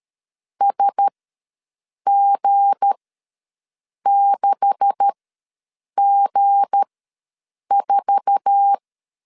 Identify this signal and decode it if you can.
Morse code, heard as a keyed tone, about 13 wpm, spaced as for 7 wpm, 800 Hz. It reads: SG6G4
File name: LEZIONI / 9 286 ws 12wpm 3351